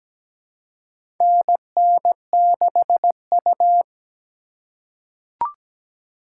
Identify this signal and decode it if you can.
Morse code, heard as a keyed tone, about 17 wpm, 700 Hz. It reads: NN6U
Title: USA Call Signs - 2x Character Spacing - 17wpm